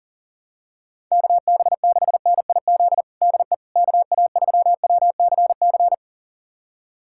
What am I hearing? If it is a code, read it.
Morse code, heard as a keyed tone, about 40 wpm, 700 Hz. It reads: KB6NIZ DE KA3WCC
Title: USA Callsign DE Callsign - 6 Characters - 40wpm